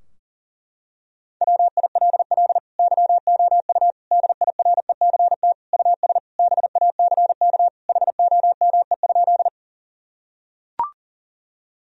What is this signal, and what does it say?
Morse code, heard as a keyed tone, about 40 wpm, 700 Hz. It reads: WILL YOU DIRECT US BACK HOME?
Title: Sentences from Top 1000 Words - Review Part 2 - 40wpm